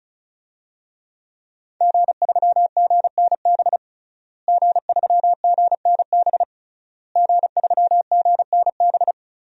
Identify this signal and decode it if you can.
Morse code, heard as a keyed tone, about 34 wpm, 700 Hz. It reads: G3GNB G3GNB G3GNB
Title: DXCC-100 Call Signs - 5 Characters - Repeated 3x - 35wpm